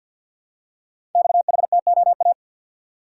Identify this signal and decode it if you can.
Morse code, heard as a keyed tone, about 51 wpm, 700 Hz. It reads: XHTOA
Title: Code Groups - 5 Letters - 50wpm